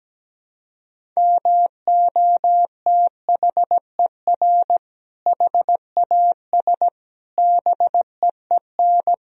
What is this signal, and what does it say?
Morse code, heard as a keyed tone, about 17 wpm, 700 Hz. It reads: MOTHER HAS BEEN
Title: Sentences from Top 1000 Words - Review Part 1 - 17wpm